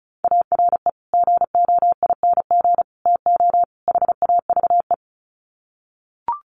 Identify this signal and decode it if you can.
Morse code, heard as a keyed tone, about 34 wpm, 700 Hz. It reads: ARE GOING TO HAVE
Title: Sets of 4 Words 35wpm